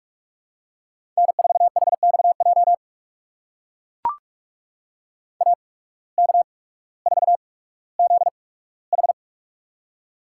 Morse code, heard as a keyed tone, about 44 wpm, spaced as for 19 wpm, 700 Hz. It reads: N4HXJ AK4ZH
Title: USA Call Signs - 5 Characters - 8x Character Spacing - 45wpm